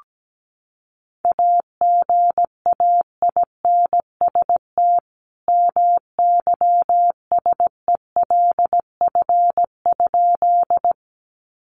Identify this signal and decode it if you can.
Morse code, heard as a keyed tone, about 17 wpm, 700 Hz. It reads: AGAINST MYSELF?
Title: Sentences from Top 1300 Words 17wpm